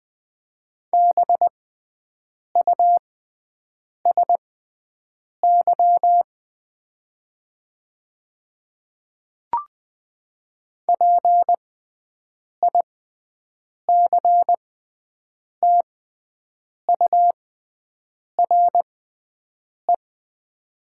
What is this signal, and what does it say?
Morse code, heard as a keyed tone, about 20 wpm, spaced as for 7 wpm, 700 Hz. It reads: BUSY PICTURE
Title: 6x Character Spacing - Top 500 Words - 20wpm